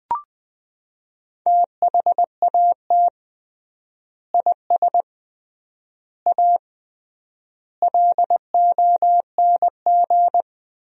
Morse code, heard as a keyed tone, about 20 wpm, 700 Hz. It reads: THAT IS A LONG
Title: 3x Word-Spacing - Sentences from Top 100 Words - 20wpm